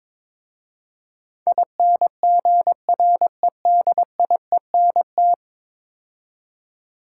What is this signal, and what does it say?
Morse code, heard as a keyed tone, about 22 wpm, 700 Hz. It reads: INGREDIENT
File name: practice / 22wpm / 10 Letter Words - First 3 Letters Given - 22wpm